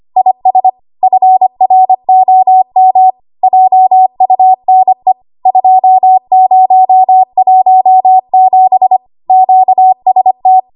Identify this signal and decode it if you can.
Morse code, heard as a keyed tone, about 25 wpm, 750 Hz. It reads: IS FROM JUNE 2017 QST
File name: Archive / 25 WPM / 25WPM